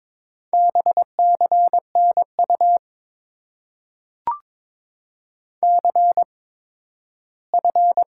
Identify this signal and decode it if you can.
Morse code, heard as a keyed tone, about 22 wpm, spaced as for 13 wpm, 700 Hz. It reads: BCNU CF
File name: Top 90 QSO Elements - 8x Character Spacing - 22wpm